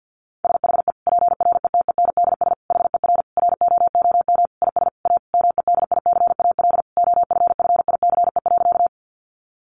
Morse code, heard as a keyed tone, about 50 wpm, 700 Hz. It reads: SHE PRETENDS HER ROOM IS A MEDICAL OFFICE.
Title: Additional Easy Phrases 50wpm